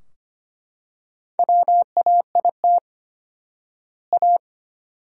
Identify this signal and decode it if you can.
Morse code, heard as a keyed tone, about 25 wpm, 700 Hz. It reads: WAIT A
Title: 4x Word-Spacing - Sentences from Top 500 Words - 25wpm